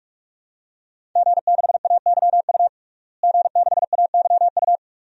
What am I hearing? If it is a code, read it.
Morse code, heard as a keyed tone, about 46 wpm, 700 Hz. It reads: G6AYU G6AYU
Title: DXCC-100 Call Signs - 5 Characters - Repeated 3x - 45wpm